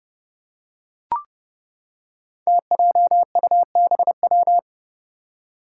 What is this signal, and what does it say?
Morse code, heard as a keyed tone, about 30 wpm, 700 Hz. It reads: TJUBW
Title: Five Challenging Letter-Number Combinations 30wpm